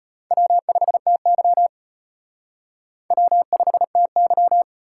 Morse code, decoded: W5TY W5TY

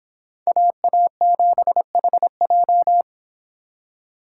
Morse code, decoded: AA7HJ

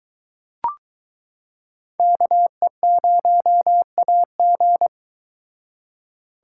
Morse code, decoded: KE0AG

